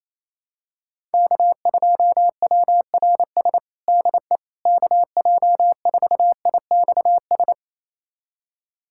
Morse code, heard as a keyed tone, about 28 wpm, 700 Hz. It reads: K2WRS DE KJ4IXS
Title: USA Callsign DE Callsign 28wpm